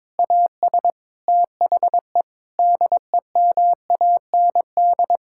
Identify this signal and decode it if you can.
Morse code, heard as a keyed tone, about 22 wpm, 700 Hz. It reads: AS THE DEMAND
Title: Sentences from Top 1000 Words - Review Part 6 - 22wpm